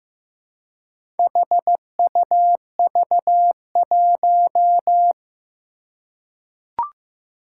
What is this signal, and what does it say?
Morse code, heard as a keyed tone, about 15 wpm, 700 Hz. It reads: HUV1
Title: Four Challenging Letter-Number Combinations 15wpm